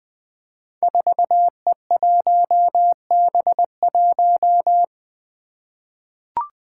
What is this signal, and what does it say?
Morse code, heard as a keyed tone, about 20 wpm, 700 Hz. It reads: 4E1B1
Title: Five Challenging Letter-Number Combinations 20wpm